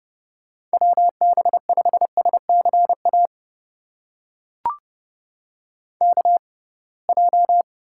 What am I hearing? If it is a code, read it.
Morse code, heard as a keyed tone, about 30 wpm, spaced as for 23 wpm, 700 Hz. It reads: WB5SCA KJ
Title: USA Call Signs - 6 Characters - 6x Character Spacing - 30wpm